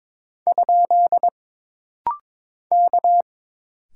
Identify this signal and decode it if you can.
Morse code, heard as a keyed tone, about 22 wpm, 700 Hz. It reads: ? K
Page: This rapid-fire timing is geared to push your ICR (Instant Character Recognition) skills.